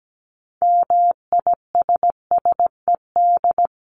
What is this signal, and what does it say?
Morse code, heard as a keyed tone, about 17 wpm, 700 Hz. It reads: MISSED
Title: Common QSO Phrases 17wpm